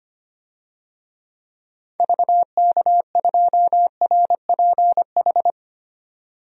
Morse code, heard as a keyed tone, about 25 wpm, 700 Hz. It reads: VK2RPH
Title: DXCC-100 Call Signs - 6 Characters - 25wpm